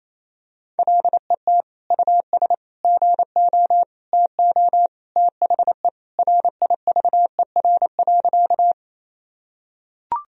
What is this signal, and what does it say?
Morse code, heard as a keyed tone, about 28 wpm, 700 Hz. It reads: LET US GO TO THE RIVER.
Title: Sentences from Top 1000 Words - Review Part 1 - 28wpm